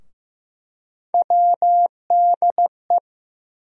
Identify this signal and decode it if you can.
Morse code, heard as a keyed tone, about 15 wpm, 700 Hz. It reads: WDE
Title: USA Call Sign Postfixes - 3 Letters 15wpm